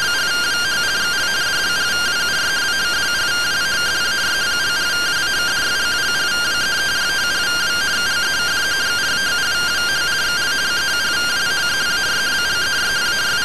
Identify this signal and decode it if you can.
Two-tone teleprinter signal, about 170 Hz shift.